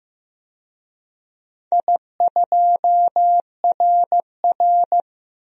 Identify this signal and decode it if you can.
Morse code, heard as a keyed tone, about 15 wpm, 700 Hz. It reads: I2RR
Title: DXCC-100 Call Signs - 4 Characters - Repeated 3x - 15wpm